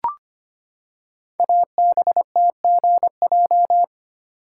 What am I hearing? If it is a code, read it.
Morse code, heard as a keyed tone, about 25 wpm, 700 Hz. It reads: ABTGJ
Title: Code Groups - 5 Letters - 25wpm